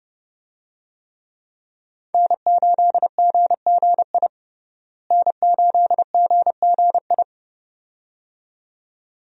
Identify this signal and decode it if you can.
Morse code, heard as a keyed tone, about 30 wpm, 700 Hz. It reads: N8GGI N8GGI
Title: USA Call Signs - 5 Characters - Repeated 2x - 30wpm